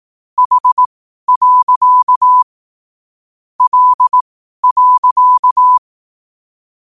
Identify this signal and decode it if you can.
Morse code, heard as a keyed tone, about 18 wpm, spaced as for 12 wpm, 1000 Hz. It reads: H. L.
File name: qotdfiles / 12wpm